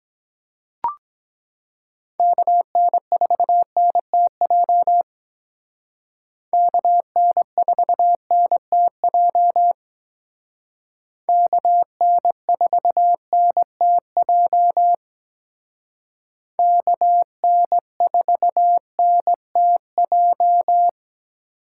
Morse code, heard as a keyed tone, about 22 wpm, 700 Hz. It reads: KN4NTJ KN4NTJ KN4NTJ KN4NTJ